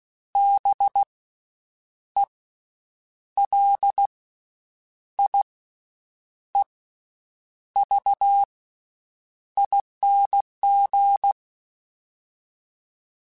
Morse code, heard as a keyed tone, about 16 wpm, spaced as for 7 wpm, 800 Hz. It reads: BELIEVING
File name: Special INGs / Special INGs-mp3s / VerySlowToFaster Ing @15